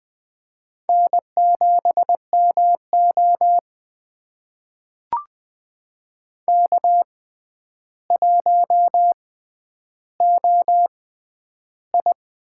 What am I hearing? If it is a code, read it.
Morse code, heard as a keyed tone, about 20 wpm, spaced as for 10 wpm, 700 Hz. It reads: N7MO K1OI
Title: USA Call Signs - 4 Characters - 6x Character Spacing - 20wpm